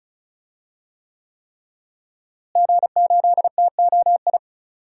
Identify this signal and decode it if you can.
Morse code, heard as a keyed tone, about 34 wpm, 700 Hz. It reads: G8TOI
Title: DXCC-100 Call Signs - 5 Characters - Repeated 3x - 35wpm